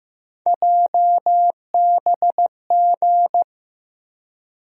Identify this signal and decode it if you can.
Morse code, heard as a keyed tone, about 15 wpm, 700 Hz. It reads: JBG
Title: USA Call Sign Postfixes - 3 Letters 15wpm